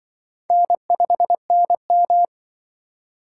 Morse code, heard as a keyed tone, about 24 wpm, 700 Hz. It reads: N5NM
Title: Speed Racing - USA 1x2 Call Signs 20wpm